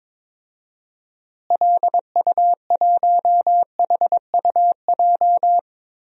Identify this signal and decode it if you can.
Morse code, heard as a keyed tone, about 22 wpm, 700 Hz. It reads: LU1HUJ
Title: DXCC-100 Call Signs - 6 Characters - Repeated 3x - 22wpm